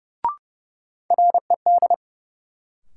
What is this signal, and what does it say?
Morse code, heard as a keyed tone, about 31 wpm, 700 Hz. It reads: RED
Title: Sentences from Top 700 Words - New Word Drill - 30wpm